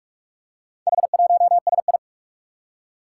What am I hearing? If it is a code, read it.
Morse code, heard as a keyed tone, about 42 wpm, 700 Hz. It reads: H1SI
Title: Four Challenging Letter-Number Combinations 45wpm